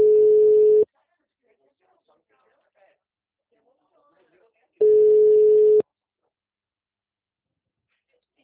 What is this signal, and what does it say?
Ringback tone (Europe, 425 Hz), with speech.